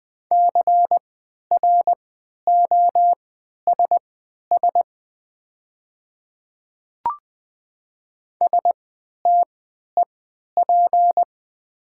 Morse code, heard as a keyed tone, about 20 wpm, spaced as for 11 wpm, 690 Hz. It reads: CROSS STEP
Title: 3x Character Spacing - Top 500 Words - 20wpm